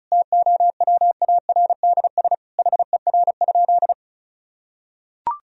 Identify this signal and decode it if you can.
Morse code, heard as a keyed tone, about 36 wpm, 700 Hz. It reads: TOWARDS HER?
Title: Sentences from Top 1000 Words - Review Part 4 - 35wpm